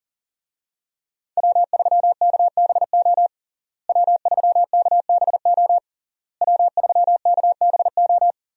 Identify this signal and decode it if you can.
Morse code, heard as a keyed tone, about 37 wpm, 700 Hz. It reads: W3KBO W3KBO W3KBO
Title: USA Call Signs - 5 Characters - Repeated 3x - 40wpm